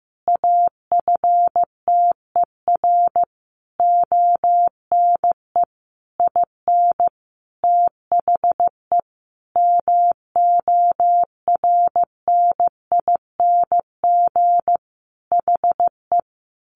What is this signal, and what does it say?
Morse code, heard as a keyed tone, about 15 wpm, 700 Hz. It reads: AFTER ONE IN THE MORNING HE
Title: Sentences from Top 1300 Words 15wpm